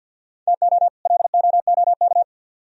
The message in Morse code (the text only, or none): TO LOOK